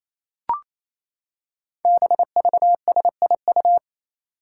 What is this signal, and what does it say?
Morse code, heard as a keyed tone, about 28 wpm, 700 Hz. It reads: BVSIU
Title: Five Challenging Letter-Number Combinations 28wpm